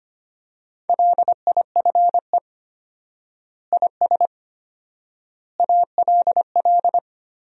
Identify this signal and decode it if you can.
Morse code, heard as a keyed tone, about 25 wpm, 700 Hz. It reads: LIFE IS ALL
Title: 4x Word-Spacing - Sentences from Top 300 Words - 25wpm